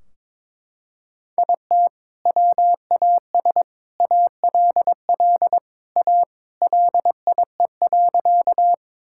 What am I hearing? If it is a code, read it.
Morse code, heard as a keyed tone, about 21 wpm, 700 Hz. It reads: IT WAS ALL A LIE.
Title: Sentences from Top 1600 Words 22wpm